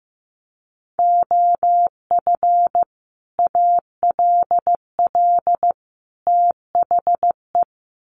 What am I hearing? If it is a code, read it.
Morse code, heard as a keyed tone, about 15 wpm, 700 Hz. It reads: OF ALL THE
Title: Sets of 3 Words 15wpm